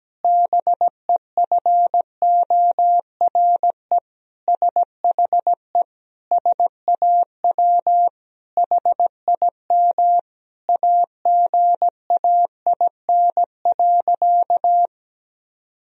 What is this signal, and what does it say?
Morse code, heard as a keyed tone, about 17 wpm, 700 Hz. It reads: BEFORE SHE SAW HIM AGAIN.
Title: Sentences from Top 1000 Words - Review Part 1 - 17wpm